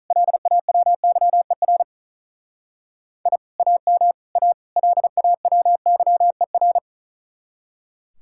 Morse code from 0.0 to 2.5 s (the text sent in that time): LAWYER